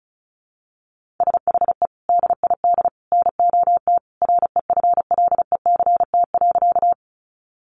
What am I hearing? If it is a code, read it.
Morse code, heard as a keyed tone, about 34 wpm, 700 Hz. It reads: SHE DID NOT REFLECT.
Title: Sentences from Top 1500 Words 35wpm